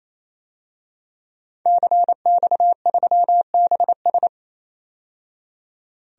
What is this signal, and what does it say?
Morse code, heard as a keyed tone, about 28 wpm, 700 Hz. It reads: CX3BS
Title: DXCC-100 Call Signs - 5 Characters - 28wpm